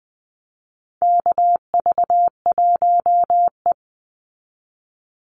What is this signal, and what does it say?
Morse code, heard as a keyed tone, about 20 wpm, 700 Hz. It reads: KV1E
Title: Call Signs Repeated 2x - 20wpm